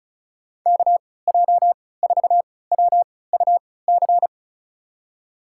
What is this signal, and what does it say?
Morse code, heard as a keyed tone, about 36 wpm, spaced as for 20 wpm, 700 Hz. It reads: KJ4WUC
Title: USA Call Signs - 6 Characters - 3x Character Spacing - 35wpm